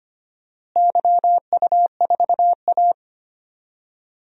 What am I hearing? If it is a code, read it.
Morse code, heard as a keyed tone, about 25 wpm, 700 Hz. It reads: YU4A
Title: DXCC-100 Call Signs - 4 Characters - Repeated 3x - 25wpm